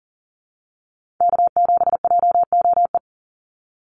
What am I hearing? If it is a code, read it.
Morse code, heard as a keyed tone, about 41 wpm, 700 Hz. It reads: K7JOE
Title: Call Signs - 40wpm